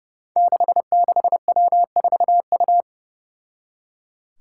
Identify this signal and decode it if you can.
Morse code, heard as a keyed tone, about 30 wpm, 700 Hz. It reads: 66W4U
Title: Five Challenging Letter-Number Combinations 30wpm